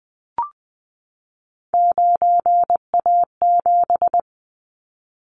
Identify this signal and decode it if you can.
Morse code, heard as a keyed tone, about 20 wpm, 700 Hz. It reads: 9A7
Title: Callsign Prefixes for Top 100 DXCC - 20wpm